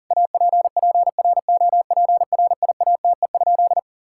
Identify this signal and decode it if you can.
Morse code, heard as a keyed tone, about 40 wpm, 700 Hz. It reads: APPROPRIATE?
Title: Sentences from Top 1000 Words - Review Part 5 - 40wpm